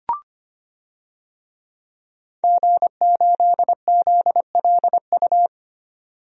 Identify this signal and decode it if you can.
Morse code, heard as a keyed tone, about 25 wpm, 700 Hz. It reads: G8ZLU
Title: DXCC-100 Call Signs - 5 Characters - 25wpm